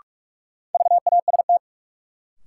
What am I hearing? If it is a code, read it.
Morse code, heard as a keyed tone, about 46 wpm, 700 Hz. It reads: VAST
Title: Sets of 4 Words - New Word Drill - 45wpm